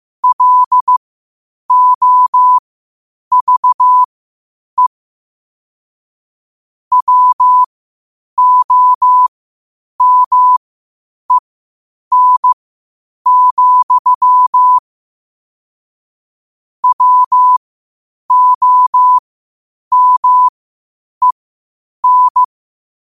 Morse code, heard as a keyed tone, about 15 wpm, spaced as for 8 wpm, 1000 Hz. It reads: LOVE WOMEN, WOMEN
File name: 8wpm